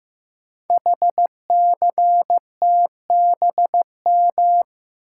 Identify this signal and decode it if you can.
Morse code, heard as a keyed tone, about 15 wpm, 700 Hz. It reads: HCTBM
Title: Code Groups - 5 Letters - 15wpm